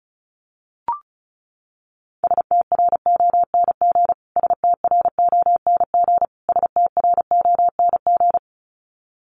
Morse code, heard as a keyed tone, about 34 wpm, 700 Hz. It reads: STRONG STRONG STRONG
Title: Top 500 Words Repeated 35wpm